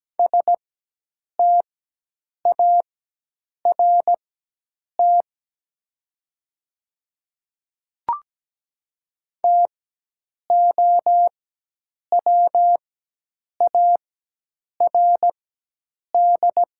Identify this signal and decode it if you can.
Morse code, heard as a keyed tone, about 17 wpm, spaced as for 8 wpm, 700 Hz. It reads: START TOWARD
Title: 4x Character Spacing - Top 500 Words - 17wpm